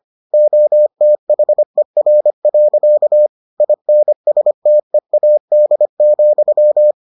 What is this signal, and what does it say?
Morse code, heard as a keyed tone, about 25 wpm, 600 Hz. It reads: OTHER. INSTEAD,